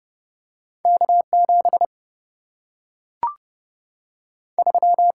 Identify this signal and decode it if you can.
Morse code, heard as a keyed tone, about 30 wpm, 700 Hz. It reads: K 7 3
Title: Single-Letter Single-Number 30wpm